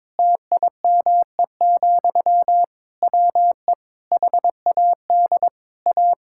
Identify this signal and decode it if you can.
Morse code, heard as a keyed tone, about 22 wpm, 700 Hz. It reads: TIME, WE HAD A
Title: Sentences from Top 1000 Words - Review Part 6 - 22wpm